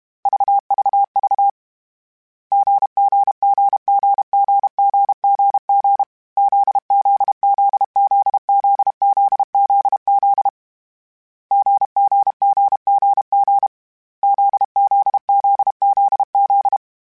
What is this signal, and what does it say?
Morse code, heard as a keyed tone, about 32 wpm, 800 Hz. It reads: VVV GGGGGGGG ZZZZZZZZ GGGGG ZZZZZ